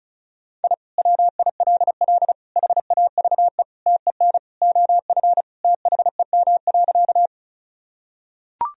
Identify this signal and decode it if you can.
Morse code, heard as a keyed tone, about 34 wpm, 700 Hz. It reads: I WILL HAVE TEN OF THEM.
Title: Sentences from Top 1000 Words - Review Part 2 - 35wpm